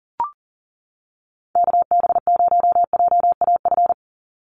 Morse code, heard as a keyed tone, about 40 wpm, 700 Hz. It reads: KB0JAF
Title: USA Call Signs 40wpm